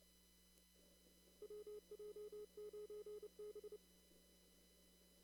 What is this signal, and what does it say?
Morse code, heard as a keyed tone, about 29 wpm, 420 Hz. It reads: WJ9B
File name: CW / kt5j wpx cw-op-k5pi